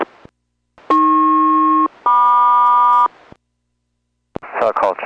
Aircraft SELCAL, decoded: AP-MQ